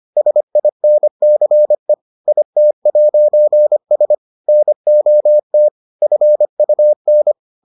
Morse code, heard as a keyed tone, about 25 wpm, 600 Hz. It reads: SINCE IT'S NOT FUN